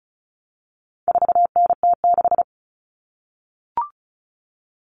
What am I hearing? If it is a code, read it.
Morse code, heard as a keyed tone, about 34 wpm, 700 Hz. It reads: 4NT6
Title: Common Cut Numbers - 4-Digit 35wpm